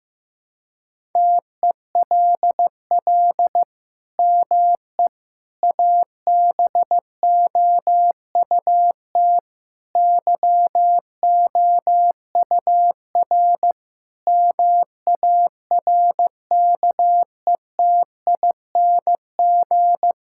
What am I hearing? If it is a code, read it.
Morse code, heard as a keyed tone, about 16 wpm, 700 Hz. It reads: TELL ME ABOUT YOUR MARKETING
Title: Sentences from Top 1000 Words - Review Part 5 - 15wpm